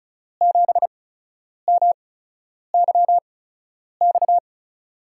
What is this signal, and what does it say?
Morse code, heard as a keyed tone, about 36 wpm, spaced as for 10 wpm, 700 Hz. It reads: 7MYX
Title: USA Call Signs - 6 Characters - 8x Character Spacing - 35wpm